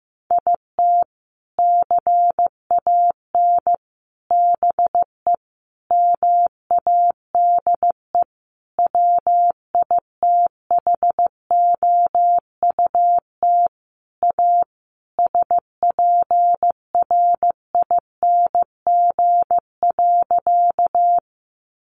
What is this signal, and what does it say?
Morse code, heard as a keyed tone, about 15 wpm, 700 Hz. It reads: IT CAN BE MADE WITHOUT A SPRING.
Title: Sentences from Top 1200 Words 15wpm